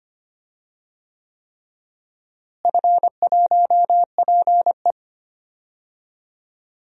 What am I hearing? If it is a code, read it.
Morse code, heard as a keyed tone, about 25 wpm, 700 Hz. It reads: F1PE
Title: DXCC-100 Call Signs - 4 Characters - Repeated 3x - 25wpm